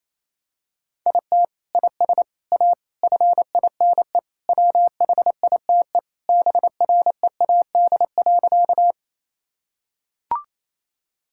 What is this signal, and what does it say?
Morse code, heard as a keyed tone, about 28 wpm, 700 Hz. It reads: IT IS A FINE WHITE BREAD.
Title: Sentences from Top 1600 Words 28wpm